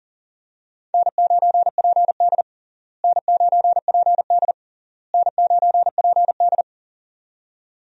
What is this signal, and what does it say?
Morse code, heard as a keyed tone, about 40 wpm, 700 Hz. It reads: N9PD N9PD N9PD
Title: USA Call Signs - 4 Characters - Repeated 3x - 40wpm